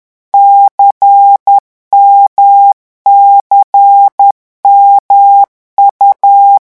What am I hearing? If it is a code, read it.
Morse code, heard as a keyed tone, about 10 wpm, 800 Hz. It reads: CMCMU
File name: morse mp3 / revision2-vitesse 10 mots